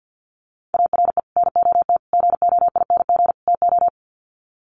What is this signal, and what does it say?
Morse code, heard as a keyed tone, about 50 wpm, 700 Hz. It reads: ARE NOT GOING TO
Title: Sets of 4 Words 50wpm